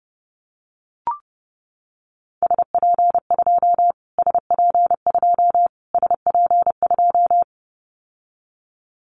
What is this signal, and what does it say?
Morse code, heard as a keyed tone, about 30 wpm, 700 Hz. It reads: SP2 SP2 SP2
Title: Callsign Prefixes for Top 25 DXCC - Repeated 3x - 30wpm